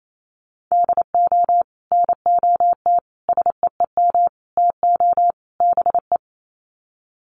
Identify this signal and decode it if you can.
Morse code, heard as a keyed tone, about 28 wpm, 700 Hz. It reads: DO NOT SEEM TO BE
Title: Sets of 5 Words Encore2 28wpm